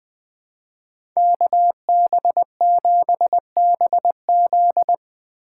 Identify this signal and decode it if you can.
Morse code, heard as a keyed tone, about 19 wpm, 700 Hz. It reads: KB7BZ